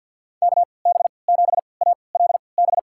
Morse code, heard as a keyed tone, about 46 wpm, spaced as for 27 wpm, 700 Hz. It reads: KB7ALB
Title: USA Call Signs - 6 Characters - 3x Character Spacing - 50wpm